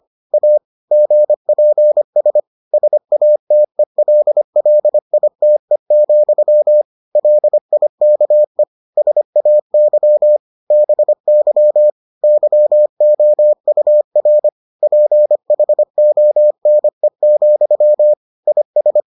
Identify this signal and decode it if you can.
Morse code, heard as a keyed tone, about 25 wpm, 600 Hz. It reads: A GPS SATELLITE, LIKE SAY BY YOUR PHONE, IS